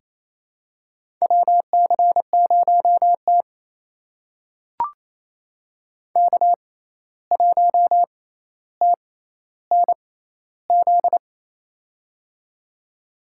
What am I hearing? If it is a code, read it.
Morse code, heard as a keyed tone, about 27 wpm, spaced as for 13 wpm, 700 Hz. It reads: WC0T K1TNZ
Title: USA Call Signs - 6x Character Spacing - 28wpm